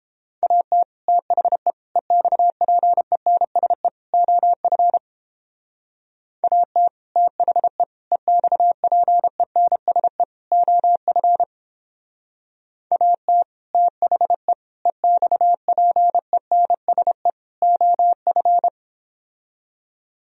Morse code, heard as a keyed tone, about 29 wpm, 700 Hz. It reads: AT THE EXPENSE OF AT THE EXPENSE OF AT THE EXPENSE OF